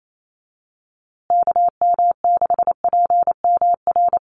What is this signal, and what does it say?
Morse code, heard as a keyed tone, about 28 wpm, 700 Hz. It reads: KM6PMR
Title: USA Call Signs Repeated 2x 28wpm